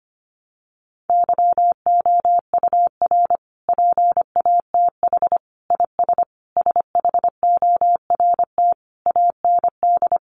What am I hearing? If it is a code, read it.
Morse code, heard as a keyed tone, about 25 wpm, 700 Hz. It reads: YOUR PATH IS SHORT AND
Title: Sentences from Top 1400 Words 25wpm